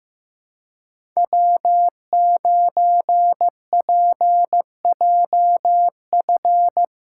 Morse code, heard as a keyed tone, about 16 wpm, 700 Hz. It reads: W9PJF
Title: USA Call Signs - 5 Characters - 2x Character Spacing - 15wpm